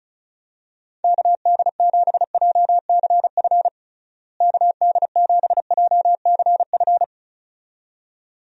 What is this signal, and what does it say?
Morse code, heard as a keyed tone, about 34 wpm, 700 Hz. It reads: KD7JCF KD7JCF
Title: USA Call Signs - 6 Characters - Repeated 2x - 35wpm